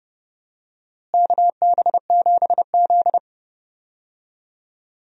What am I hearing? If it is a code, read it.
Morse code, heard as a keyed tone, about 30 wpm, 700 Hz. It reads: KB7Z
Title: USA Call Signs - 4 Characters - Repeated 2x - 30wpm